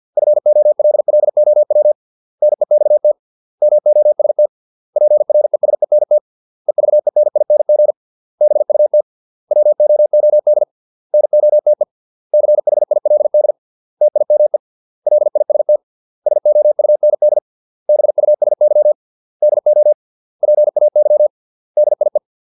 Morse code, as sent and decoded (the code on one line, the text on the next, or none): ..-. --- .-.. .-.. --- .-- / -. -..- - / -- --- ... - / .--. .-. ... -. - / . ...- . -. .. -. --. / -... ..- - / .-- --- --- -.. / -. --- - . / -.-. .... .. .-.. -.. / - .. -- . / .-.. .. ... - / ... --- ..- -. -.. / -... ..- ... -.-- / -.. --- / .--. .- -.-- / -.. .. .
FOLLOW NEXT MOST PRESENT EVENING BUT WOOD NOTE CHILD TIME LIST SOUND BUSY DO PAY DIE